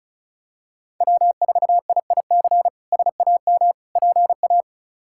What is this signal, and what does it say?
Morse code, heard as a keyed tone, about 36 wpm, 700 Hz. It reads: W4IIC SAM PA
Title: K1USN SST QSO Practice - 35wpm